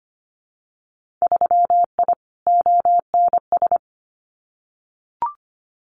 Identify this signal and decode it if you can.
Morse code, heard as a keyed tone, about 25 wpm, 700 Hz. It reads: 3I ONS
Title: Winter Field Day Exchanges Encore 25wpm